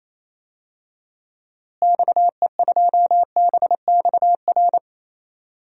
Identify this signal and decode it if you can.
Morse code, heard as a keyed tone, about 28 wpm, 700 Hz. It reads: XE2BXR
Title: DXCC-100 Call Signs - 6 Characters - Repeated 3x - 28wpm